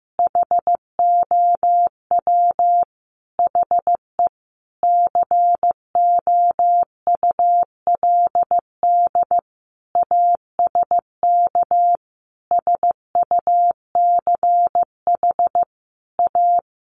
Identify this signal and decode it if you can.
Morse code, heard as a keyed tone, about 15 wpm, 700 Hz. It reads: HOW HE COULD ASK SUCH A
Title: Sentences from Top 600 Words 15wpm